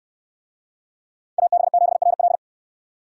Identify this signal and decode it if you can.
Morse code, heard as a keyed tone, about 64 wpm, 700 Hz. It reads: AB7RL